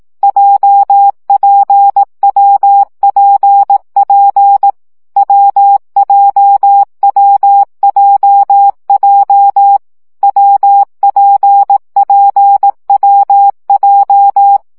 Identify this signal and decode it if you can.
Morse code, heard as a keyed tone, about 18 wpm, 800 Hz. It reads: JPWPP WJWJJ WPPWJ
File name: CW gyakorlas w p j 15 baud